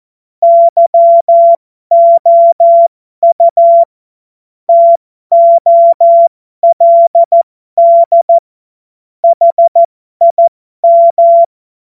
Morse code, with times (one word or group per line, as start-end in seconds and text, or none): 0.4-3.8 YOU
4.7-8.4 TOLD
9.2-11.4 HIM